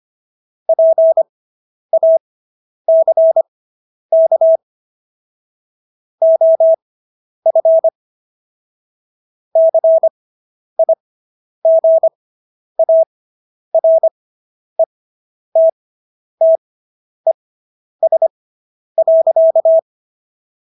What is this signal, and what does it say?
Morse code, heard as a keyed tone, about 25 wpm, spaced as for 10 wpm, 650 Hz. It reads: PACK OF CIGARETTES.